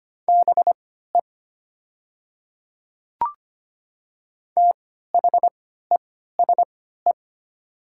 Morse code, heard as a keyed tone, about 25 wpm, spaced as for 14 wpm, 700 Hz. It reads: BE THESE
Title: 3x Character Spacing - Top 300 Words - 25wpm